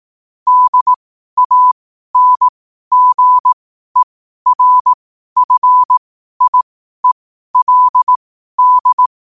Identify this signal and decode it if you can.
Morse code, heard as a keyed tone, about 18 wpm, spaced as for 12 wpm, 1000 Hz. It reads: DANGERFIELD
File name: qotdfiles / 12wpm